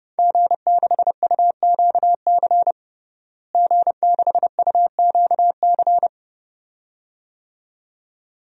Morse code, heard as a keyed tone, about 32 wpm, 700 Hz. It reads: G6UQC G6UQC
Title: DXCC-100 Call Signs - 5 Characters - Repeated 2x - 30wpm